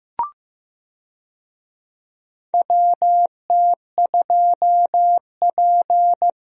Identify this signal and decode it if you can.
Morse code, heard as a keyed tone, about 15 wpm, 700 Hz. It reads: WT2P
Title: USA Call Signs - 4 Characters - 15wpm